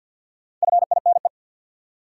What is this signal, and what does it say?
Morse code, heard as a keyed tone, about 50 wpm, 700 Hz. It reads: FINE
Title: Four Letter Words - 50wpm